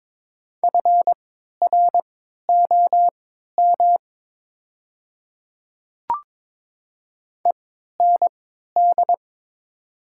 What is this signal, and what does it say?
Morse code, heard as a keyed tone, about 23 wpm, spaced as for 13 wpm, 700 Hz. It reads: FROM END
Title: 3x Character Spacing - Top 200 Words - 22wpm